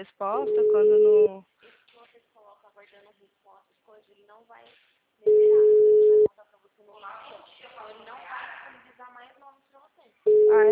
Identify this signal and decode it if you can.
Ringback tone (Europe, 425 Hz), with speech.